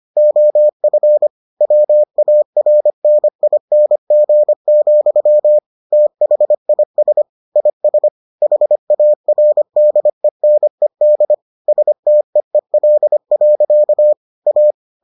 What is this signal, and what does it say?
Morse code, heard as a keyed tone, about 25 wpm, 600 Hz. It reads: OF WARNING, THIS IS HARDENED STEEL. A